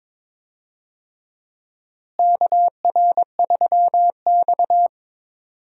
Morse code, heard as a keyed tone, about 22 wpm, 700 Hz. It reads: KR3X
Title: USA Call Signs - 4 Characters - Repeated 2x - 22wpm